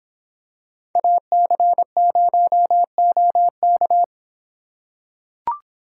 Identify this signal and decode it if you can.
Morse code, heard as a keyed tone, about 24 wpm, 700 Hz. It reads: AC0OK